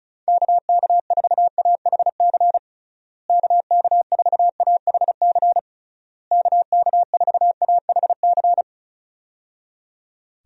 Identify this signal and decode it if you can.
Morse code, heard as a keyed tone, about 34 wpm, 700 Hz. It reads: KK4AHC KK4AHC KK4AHC
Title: USA Call Signs - 6 Characters - Repeated 3x - 35wpm